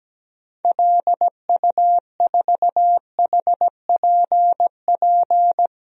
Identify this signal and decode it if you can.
Morse code, heard as a keyed tone, about 17 wpm, 700 Hz. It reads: LU4HPP
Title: DXCC-100 Call Signs - 6 Characters - 17wpm